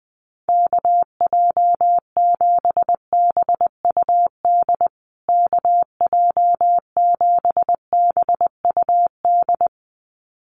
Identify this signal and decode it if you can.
Morse code, heard as a keyed tone, about 20 wpm, 700 Hz. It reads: KJ7BUD KJ7BUD
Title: USA Call Signs Repeated 2x 20wpm